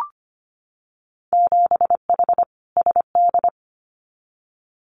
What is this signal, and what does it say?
Morse code, heard as a keyed tone, about 25 wpm, 700 Hz. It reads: 7H SD
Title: Winter Field Day Exchanges Encore 25wpm